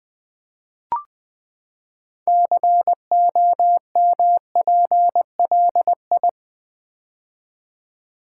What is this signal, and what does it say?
Morse code, heard as a keyed tone, about 21 wpm, 700 Hz. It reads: COMPLI
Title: Build A Word - 6 Letters At A Time - 12-Letter Words - 20wpm